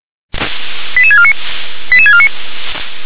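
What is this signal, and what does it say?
Five-tone selective call (ZVEI), heard as quick runs of five tones, tones of about 70 ms. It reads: 88649 88649
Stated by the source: ZVEI five-tone selective call received by SDR.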